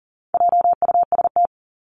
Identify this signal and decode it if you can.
Morse code, heard as a keyed tone, about 40 wpm, 700 Hz. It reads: JUST
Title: Top 500 Words Repeated 40wpm